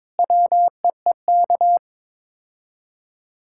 Morse code, heard as a keyed tone, about 22 wpm, 700 Hz. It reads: WEEK
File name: practice / 22wpm / Sets of 7 Words - 22wpm